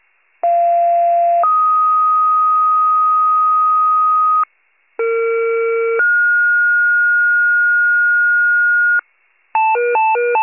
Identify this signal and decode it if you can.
Two-tone page, heard as a one-second tone, then a three-second tone, about 690 then 1250 Hz, about 460 then 1430 Hz, with noise. It is heard from 0.4 to 9.0 s.
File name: pager / 21 FIRE / wcalert